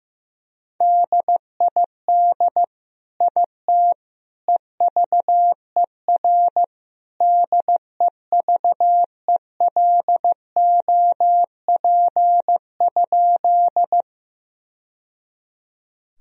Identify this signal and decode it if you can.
Morse code, heard as a keyed tone, about 16 wpm, 700 Hz. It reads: DID IT EVER DEVELOP?
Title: Sentences from Top 1000 Words - Review Part 2 - 15wpm